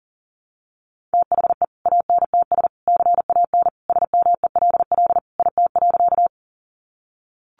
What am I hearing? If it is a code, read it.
Morse code, heard as a keyed tone, about 40 wpm, 700 Hz. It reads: THE ANTS CAN SMELL IT.
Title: Easy Phrases 40wpm